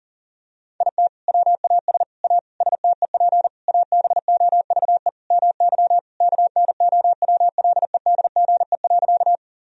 Morse code, heard as a keyed tone, about 40 wpm, 700 Hz. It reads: IT WAS A STEP ABOVE MY KNOWLEDGE.
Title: Sentences from Top 1000 Words - Review Part 3 - 40wpm